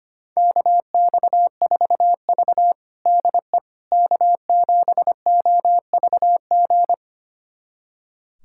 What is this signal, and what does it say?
Morse code, heard as a keyed tone, about 25 wpm, 700 Hz. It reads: KX4V DE K7OVG
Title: USA Callsign DE Callsign 25wpm